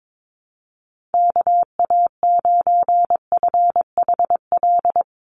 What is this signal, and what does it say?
Morse code, heard as a keyed tone, about 22 wpm, 700 Hz. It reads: KA9FHL
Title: USA Call Signs 22wpm